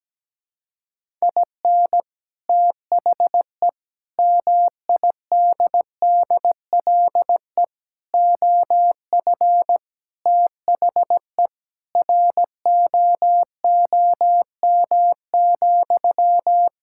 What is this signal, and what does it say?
Morse code, heard as a keyed tone, about 17 wpm, 700 Hz. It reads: IN THE MIDDLE OF THE ROOM,
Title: Sentences from Top 1000 Words - Review Part 6 - 17wpm